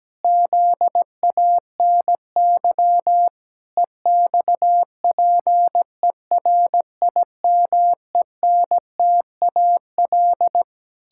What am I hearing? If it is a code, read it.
Morse code, heard as a keyed tone, about 17 wpm, 700 Hz. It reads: ZANY EXPERIMENTAL
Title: Pangram Challenge v2 - 17wpm